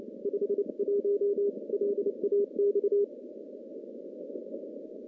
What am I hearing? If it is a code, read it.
Morse code, heard as a keyed tone, about 28 wpm, 410 Hz. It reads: 51 LAX
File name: contests / CW-SS-k5tr- -0000